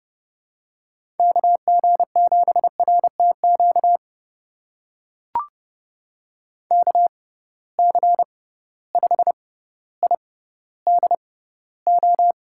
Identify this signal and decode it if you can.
Morse code, heard as a keyed tone, about 29 wpm, spaced as for 15 wpm, 700 Hz. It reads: KG7RTQ KC5IDO
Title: USA Call Signs - 6 Characters - 6x Character Spacing - 30wpm